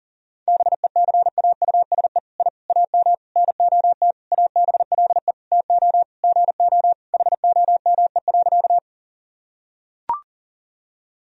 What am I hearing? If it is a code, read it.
Morse code, heard as a keyed tone, about 40 wpm, 700 Hz. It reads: BECAUSE I AM NOT ABLE TO GO HOME.
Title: Sentences from Top 1000 Words - Review Part 3 - 40wpm